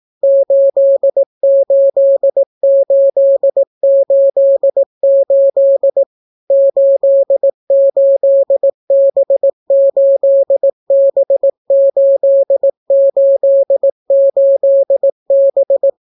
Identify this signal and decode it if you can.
Morse code, heard as a keyed tone, about 18 wpm, 550 Hz. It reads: 88888 88B8B888B